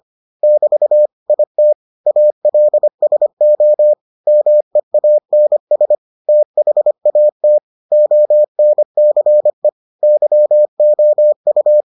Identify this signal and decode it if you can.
Morse code, heard as a keyed tone, about 25 wpm, 600 Hz. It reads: <BT> IT ALSO MEANS THAT ONCE YOU